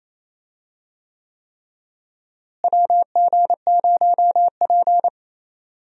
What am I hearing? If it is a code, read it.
Morse code, heard as a keyed tone, about 28 wpm, 700 Hz. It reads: WG0P
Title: USA Call Signs - 4 Characters - Repeated 3x - 28wpm